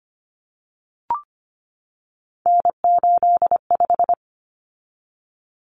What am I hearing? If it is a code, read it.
Morse code, heard as a keyed tone, about 25 wpm, 700 Hz. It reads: N85
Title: Common Cut Numbers - 3-Digit 25wpm